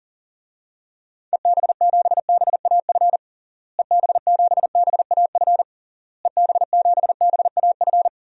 Morse code, heard as a keyed tone, about 40 wpm, 700 Hz. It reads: EB7BAF EB7BAF EB7BAF
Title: DXCC-100 Call Signs - 6 Characters - Repeated 3x - 40wpm